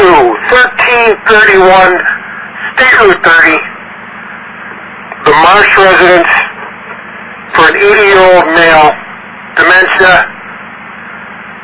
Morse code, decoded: none

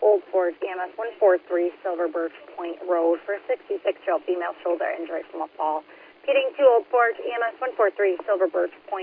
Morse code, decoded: none

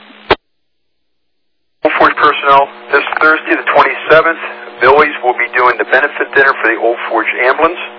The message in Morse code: none

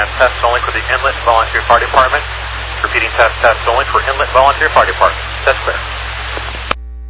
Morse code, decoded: none

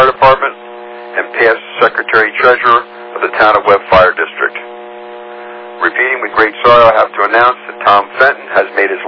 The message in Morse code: none

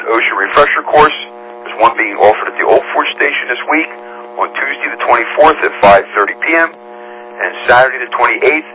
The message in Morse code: none

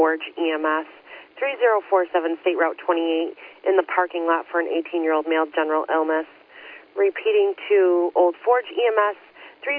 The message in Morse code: none

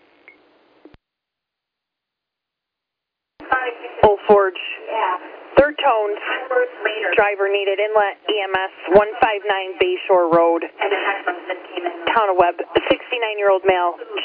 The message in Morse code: none

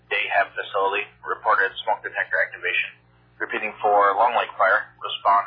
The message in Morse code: none

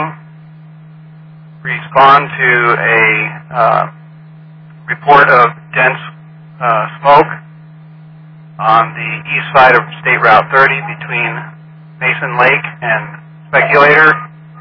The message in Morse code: none

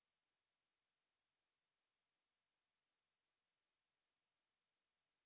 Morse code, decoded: none

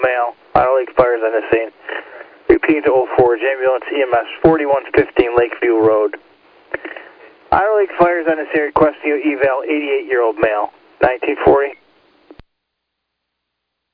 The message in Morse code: none

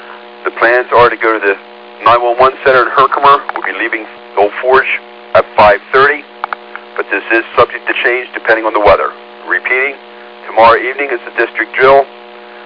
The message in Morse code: none

